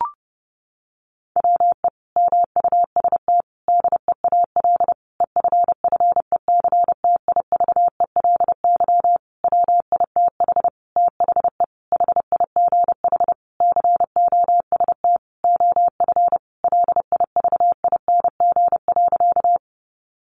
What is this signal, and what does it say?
Morse code, heard as a keyed tone, about 30 wpm, 700 Hz. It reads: WE MUST DEAL EFFECTIVELY WITH THE HIGH COST OF LIVING.
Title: Sentences from Top 1400 Words 30wpm